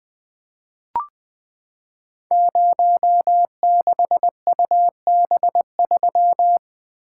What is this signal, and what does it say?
Morse code, heard as a keyed tone, about 20 wpm, 700 Hz. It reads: 06UB3
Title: Code Groups - 5 Letter:Numbers - 20wpm